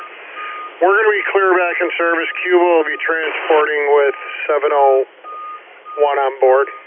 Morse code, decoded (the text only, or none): none